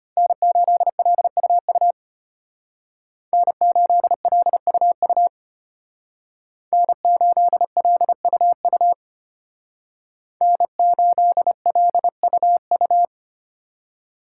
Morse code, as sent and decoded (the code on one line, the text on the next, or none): -. ---.. .-.. ..- ..- / -. ---.. .-.. ..- ..- / -. ---.. .-.. ..- ..- / -. ---.. .-.. ..- ..-
N8LUU N8LUU N8LUU N8LUU